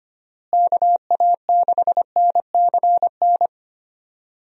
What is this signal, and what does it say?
Morse code, heard as a keyed tone, about 25 wpm, 700 Hz. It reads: KA6NCN
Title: USA Call Signs - 6 Characters - Repeated 2x - 25wpm